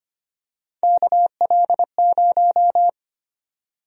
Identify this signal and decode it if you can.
Morse code, heard as a keyed tone, about 25 wpm, 700 Hz. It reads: KL0
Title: USA Call Sign Prefixes - 25wpm